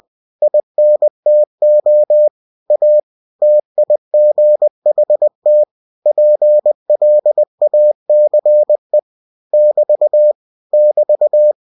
Morse code, heard as a keyed tone, about 20 wpm, 600 Hz. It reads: INTO A TIGHT PLACE <BT> <BT>